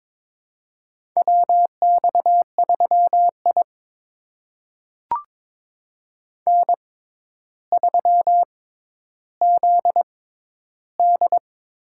Morse code, heard as a keyed tone, about 23 wpm, spaced as for 11 wpm, 700 Hz. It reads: WX3I N3ZD
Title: USA Call Signs - 4 Characters - 6x Character Spacing - 22wpm